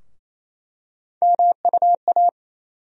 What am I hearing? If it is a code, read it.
Morse code, heard as a keyed tone, about 28 wpm, 700 Hz. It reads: MUA
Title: USA Call Sign Postfixes - 3 Letters 28wpm